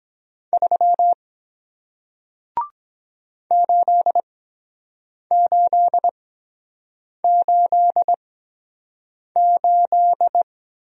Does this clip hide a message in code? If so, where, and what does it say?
Yes: Morse code from 0.5 to 10.4 s, heard as a keyed tone, about 23 wpm, spaced as for 7 wpm, 700 Hz. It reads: 3 8888